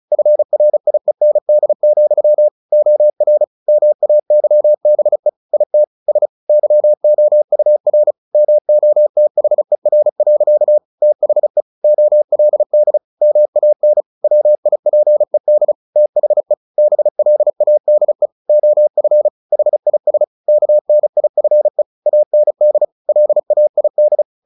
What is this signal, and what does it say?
Morse code, heard as a keyed tone, about 34 wpm, 600 Hz. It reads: FRIEND, OR MAYBE IT S YOUR MOTHER. THE OLD MAN WIPED THE BLADE OF HIS KNIFE AND LAID
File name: the old man and the sea 2 35wpm